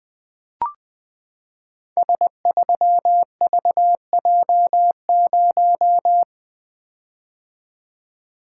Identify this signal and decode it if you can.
Morse code, heard as a keyed tone, about 20 wpm, 700 Hz. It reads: S3VJ0
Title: Code Groups - 5 Letter:Numbers - 20wpm